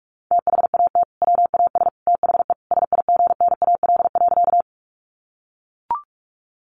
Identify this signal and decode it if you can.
Morse code, heard as a keyed tone, about 44 wpm, 700 Hz. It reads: THAT WAS THE SIGNAL.
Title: Sentences from Top 1400 Words 45wpm